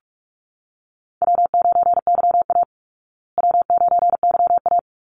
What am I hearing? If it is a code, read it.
Morse code, heard as a keyed tone, about 44 wpm, 700 Hz. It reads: W9YA W9YA
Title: Easy Call Signs Repeated 3x - 45wpm